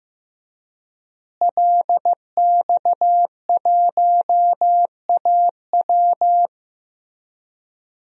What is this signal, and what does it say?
Morse code, heard as a keyed tone, about 15 wpm, 700 Hz. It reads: LX1AW
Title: DXCC-100 Call Signs - 5 Characters - Repeated 3x - 15wpm